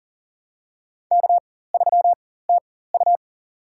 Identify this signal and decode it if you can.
Morse code, heard as a keyed tone, about 37 wpm, spaced as for 18 wpm, 700 Hz. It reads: K3TU
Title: USA Call Signs - 4 Characters - 4x Character Spacing - 40wpm